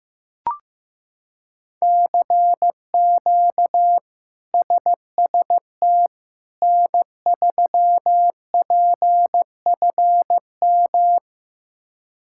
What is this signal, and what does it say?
Morse code, heard as a keyed tone, about 15 wpm, 700 Hz. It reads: CQ SST N3PFM
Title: K1USN SST QSO Practice - 15wpm